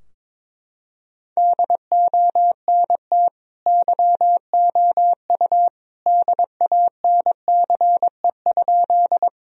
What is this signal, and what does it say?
Morse code, heard as a keyed tone, about 22 wpm, 700 Hz. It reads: DONT YOU DANCE?
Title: Sentences from Top 1000 Words - Review Part 4 - 22wpm